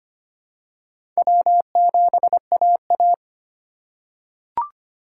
Morse code, heard as a keyed tone, about 25 wpm, 700 Hz. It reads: W7AA
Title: USA Call Signs - 4 Characters - 3x Character Spacing - 25wpm